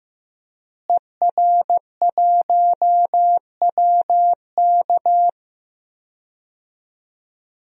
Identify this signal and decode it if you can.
Morse code, heard as a keyed tone, about 15 wpm, 700 Hz. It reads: ER1WK
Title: DXCC-100 Call Signs - 5 Characters - Repeated 3x - 15wpm